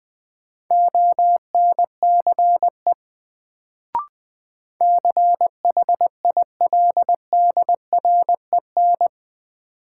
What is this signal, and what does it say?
Morse code, heard as a keyed tone, about 20 wpm, 700 Hz. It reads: ONCE CHILDREN
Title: Sentences from Top 300 Words - New Word Drill - 20wpm